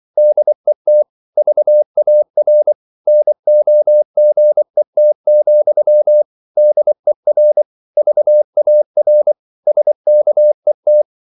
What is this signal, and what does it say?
Morse code, heard as a keyed tone, about 24 wpm, 600 Hz. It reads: DET VAR NOGET, DER VAR SKET
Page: Ferie 24 wpm